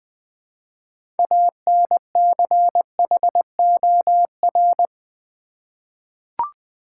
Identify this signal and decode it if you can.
Morse code, heard as a keyed tone, about 20 wpm, 700 Hz. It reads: ANCHOR
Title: 6 Letter Words - First 3 Letters Given - 20wpm